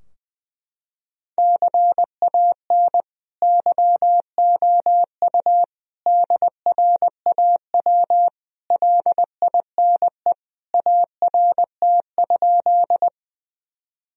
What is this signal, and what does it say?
Morse code, heard as a keyed tone, about 20 wpm, 700 Hz. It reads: CAN YOU DRAW LINE ART?
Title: Sentences from Top 1000 Words - Review Part 3 - 20wpm